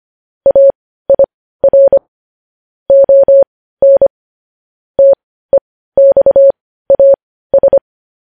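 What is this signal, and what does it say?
Morse code, heard as a keyed tone, about 25 wpm, spaced as for 15 wpm, 550 Hz. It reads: AIR ON TEXAS
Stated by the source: Morse code news headlines at 15 wpm.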